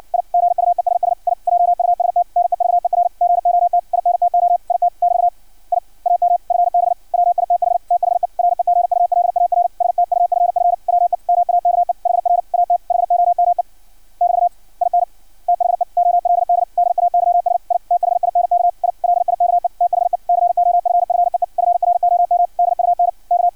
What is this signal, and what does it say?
Morse code, heard as a keyed tone, about 56 wpm, 700 Hz. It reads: I GUESS I JUST NEVER GOT INTO IT <BT> I AM FB WITH THE REGULAR STUFF WE MAKE HR AT HOME <BT> IN THE OLD DAYS I THINK I LIKE THE COFFEE FROM DUNK
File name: archived / CRSnet